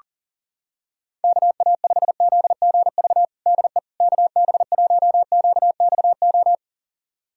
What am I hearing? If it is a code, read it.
Morse code, heard as a keyed tone, about 40 wpm, 700 Hz. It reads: KA5ZGV DE KB1QXO
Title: USA Callsign DE Callsign - 6 Characters - 40wpm